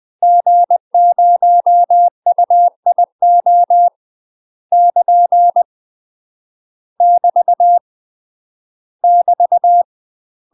Morse code, heard as a keyed tone, about 20 wpm, 700 Hz. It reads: G0UIO <KN> <BT> <BT>
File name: UK MASTER QSOs TRAINING / 20WPM / UK MASTER QSOs 0067